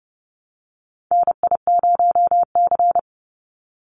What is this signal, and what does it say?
Morse code, heard as a keyed tone, about 30 wpm, 700 Hz. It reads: NI0C
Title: Call Signs Repeated 3x - 30wpm